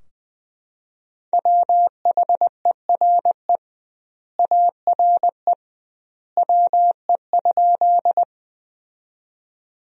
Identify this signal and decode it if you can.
Morse code, heard as a keyed tone, about 21 wpm, 700 Hz. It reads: WHERE ARE WE?
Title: 2x Word-Spacing - Sentences from Top 200 Words - 20wpm